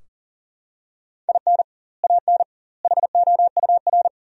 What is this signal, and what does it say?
Morse code, heard as a keyed tone, about 40 wpm, 700 Hz. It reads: IN AN HOUR
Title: 2x Word-Spacing - Sentences from Top 400 Words - 40wpm